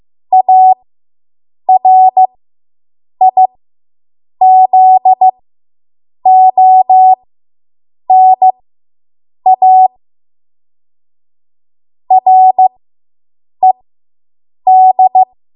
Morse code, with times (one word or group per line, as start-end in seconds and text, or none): 0.3-9.9 ARIZONA
12.1-15.2 RED